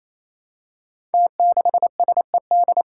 Morse code, heard as a keyed tone, about 27 wpm, 700 Hz. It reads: T6SED
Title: Five Challenging Letter-Number Combinations 28wpm